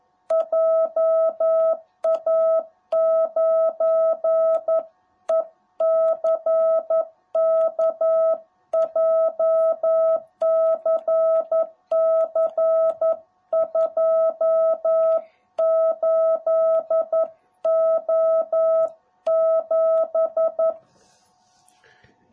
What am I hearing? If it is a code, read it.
Morse code, heard as a keyed tone, about 11 wpm, 660 Hz. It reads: JA9ECK JCC28O7
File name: jh8cbh / 2021cw / cw18b